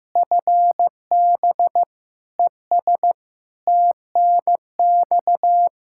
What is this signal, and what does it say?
Morse code, heard as a keyed tone, about 15 wpm, 700 Hz. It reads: FB ES TNX
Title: K1USN SST QSO Practice 15wpm